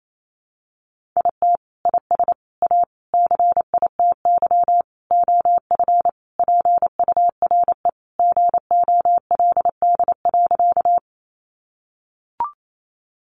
Morse code, heard as a keyed tone, about 28 wpm, 700 Hz. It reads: IT IS A CITY OF PURE GOLD.
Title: Sentences from Top 1500 Words 28wpm